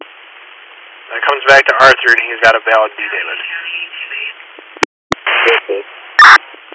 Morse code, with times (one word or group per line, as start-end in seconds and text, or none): none